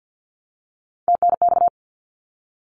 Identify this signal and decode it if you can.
Morse code, heard as a keyed tone, about 53 wpm, 700 Hz. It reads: TNX
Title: Abbreviations Repeated 3x - 50wpm